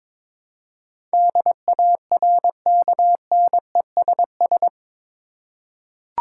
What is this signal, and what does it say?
Morse code, heard as a keyed tone, about 22 wpm, 700 Hz. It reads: DARKNESS
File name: practice / 22wpm / Eight Letter Words - 22wpm